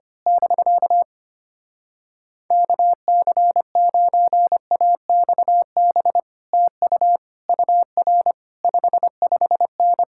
Morse code, heard as a keyed tone, about 24 wpm, 700 Hz. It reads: <BK> KC9AXB TU UR 55N
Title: Speed-Racing Common QSO Phrases - 25wpm